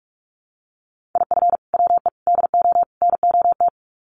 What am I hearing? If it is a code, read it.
Morse code, heard as a keyed tone, about 44 wpm, 700 Hz. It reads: IF WE DO NOT
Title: Sets of 4 Words 45wpm